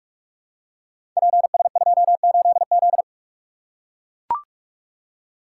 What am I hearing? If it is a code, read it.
Morse code, heard as a keyed tone, about 44 wpm, 700 Hz. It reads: PS28Z
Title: Code Groups - 5 Letter:Numbers - 45wpm